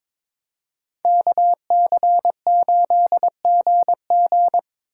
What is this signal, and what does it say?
Morse code, heard as a keyed tone, about 26 wpm, 700 Hz. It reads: KC8GG
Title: Speed Racing - USA 2x2 Call Signs 22wpm